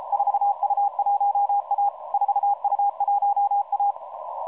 Morse code, heard as a keyed tone, about 34 wpm, 810 Hz. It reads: 4U1A 4U1A
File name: 4u1a 14016.0kHz